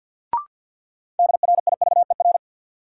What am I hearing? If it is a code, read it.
Morse code, heard as a keyed tone, about 53 wpm, 700 Hz. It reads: DRIVER
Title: Sentences from Top 1000 Words - New Word Drill - 50wpm